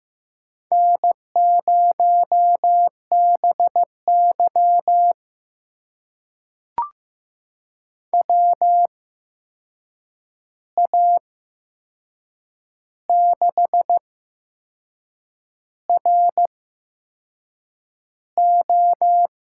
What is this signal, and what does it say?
Morse code, heard as a keyed tone, about 15 wpm, spaced as for 6 wpm, 700 Hz. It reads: N0BY WA6RO
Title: USA Call Signs - 8x Character Spacing - 15wpm